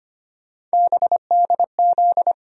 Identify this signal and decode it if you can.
Morse code, heard as a keyed tone, about 25 wpm, 700 Hz. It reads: BDZ
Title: USA Call Sign Postfixes - 3 Letters 25wpm